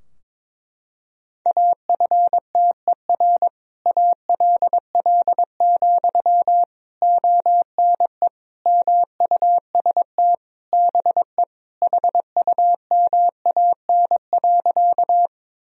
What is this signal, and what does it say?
Morse code, heard as a keyed tone, about 22 wpm, 700 Hz. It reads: AFTER ALL, ONE MUST BE HUMAN.
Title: Sentences from Top 1000 Words - Review Part 3 - 22wpm